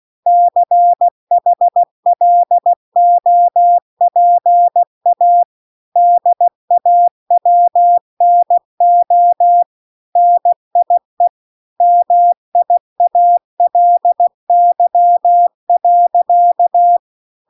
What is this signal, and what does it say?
Morse code, heard as a keyed tone, about 16 wpm, 700 Hz. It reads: CHLOPA DAWNO NIE MIALY.